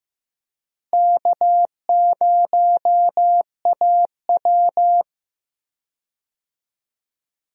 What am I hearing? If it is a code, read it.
Morse code, heard as a keyed tone, about 15 wpm, 700 Hz. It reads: K0AW
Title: USA Call Signs - 4 Characters - Repeated 3x - 15wpm